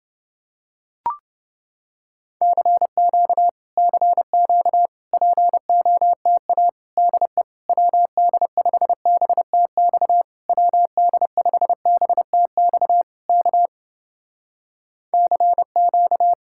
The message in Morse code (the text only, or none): CQ CQ POTA DE WD5BTX WD5BTX K CQ